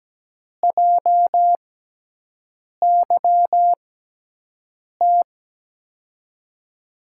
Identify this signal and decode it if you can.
Morse code, heard as a keyed tone, about 17 wpm, spaced as for 6 wpm, 700 Hz. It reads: JYT
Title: USA Call Signs - 6 Characters - 6x Character Spacing - 17wpm